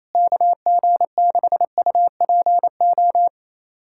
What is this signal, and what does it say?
Morse code, heard as a keyed tone, about 28 wpm, 700 Hz. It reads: KG6UPO
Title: USA Callsign DE Callsign - 6 Characters - 28wpm